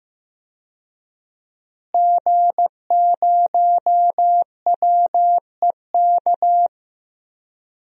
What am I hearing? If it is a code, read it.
Morse code, heard as a keyed tone, about 15 wpm, 700 Hz. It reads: G0WEK
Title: DXCC-100 Call Signs - 5 Characters - Repeated 2x - 15wpm